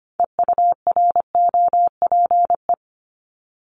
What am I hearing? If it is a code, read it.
Morse code, heard as a keyed tone, about 25 wpm, 700 Hz. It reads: EUROPE
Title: Multi-letter Phonemes 25wpm